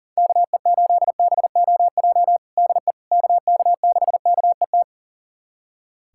Morse code, heard as a keyed tone, about 39 wpm, 700 Hz. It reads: KE8BOJ DE KK6KET
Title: USA Callsign DE Callsign - 6 Characters - 40wpm